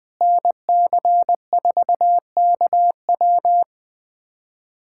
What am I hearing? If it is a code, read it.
Morse code, heard as a keyed tone, about 20 wpm, 700 Hz. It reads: NC4KW
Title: USA Callsign DE Callsign - 5 Characters - 20wpm